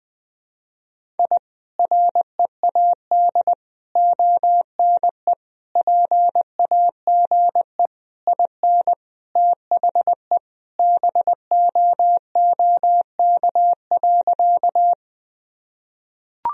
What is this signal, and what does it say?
Morse code, heard as a keyed tone, about 20 wpm, 700 Hz. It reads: I READ ONE PAGE IN THE BOOK.
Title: Sentences from Top 1000 Words - Review Part 1 - 20wpm